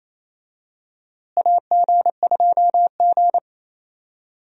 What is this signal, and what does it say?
Morse code, heard as a keyed tone, about 28 wpm, 700 Hz. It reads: AG2G
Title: USA Call Signs - 4 Characters - Repeated 3x - 28wpm